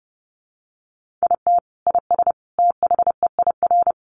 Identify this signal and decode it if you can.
Morse code, heard as a keyed tone, about 30 wpm, 700 Hz. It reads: IT IS THEIR
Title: Sentences from Top 1000 Words 30wpm